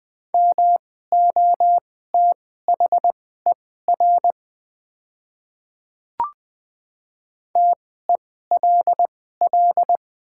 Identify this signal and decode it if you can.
Morse code, heard as a keyed tone, about 20 wpm, spaced as for 15 wpm, 700 Hz. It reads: MOTHER TELL
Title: 2x Character Spacing - Top 300 Words - 20wpm